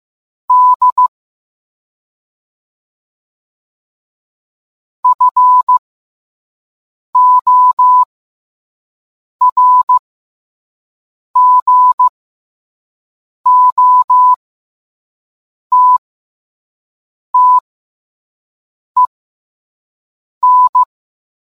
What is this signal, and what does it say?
Morse code, heard as a keyed tone, about 15 wpm, spaced as for 5 wpm, 1000 Hz. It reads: D FORGOTTEN